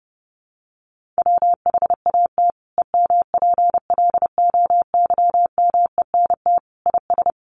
Morse code, heard as a keyed tone, about 28 wpm, 700 Hz. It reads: WHAT EMPLOYMENT IS
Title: Sentences from Top 1500 Words 30wpm